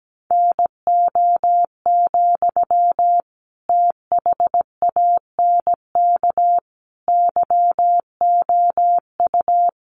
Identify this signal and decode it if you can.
Morse code, heard as a keyed tone, about 17 wpm, 700 Hz. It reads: NO, THANK YOU
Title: Sentences from Top 600 Words 17wpm